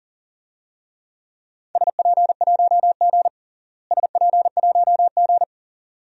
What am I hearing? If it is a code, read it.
Morse code, heard as a keyed tone, about 40 wpm, 700 Hz. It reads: SP1G SP1G
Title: DXCC-100 Call Signs - 4 Characters - Repeated 2x - 40wpm